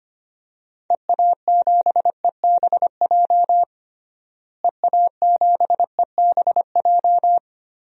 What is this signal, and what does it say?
Morse code, heard as a keyed tone, about 25 wpm, 700 Hz. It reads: EA7EBJ EA7EBJ
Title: DXCC-100 Call Signs - 6 Characters - Repeated 2x - 25wpm